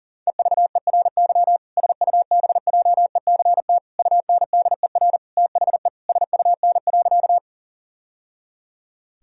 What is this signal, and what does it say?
Morse code, heard as a keyed tone, about 40 wpm, 700 Hz. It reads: EVERY SUBJECT UNDER THE SUN.
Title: Sentences from Top 1000 Words - Review Part 1 - 40wpm